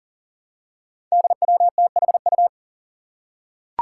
Morse code, heard as a keyed tone, about 40 wpm, 700 Hz. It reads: DWTHU
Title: Five Challenging Letter-Number Combinations 40wpm